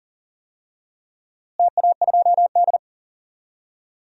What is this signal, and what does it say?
Morse code, heard as a keyed tone, about 40 wpm, 700 Hz. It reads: TA2D
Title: DXCC-100 Call Signs - 4 Characters - Repeated 3x - 40wpm